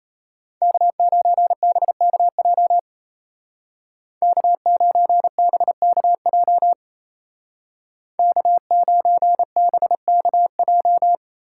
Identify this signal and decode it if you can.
Morse code, heard as a keyed tone, about 34 wpm, 700 Hz. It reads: K9BKJ K9BKJ K9BKJ
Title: Speed Racing - USA 1x3 Call Signs 28wpm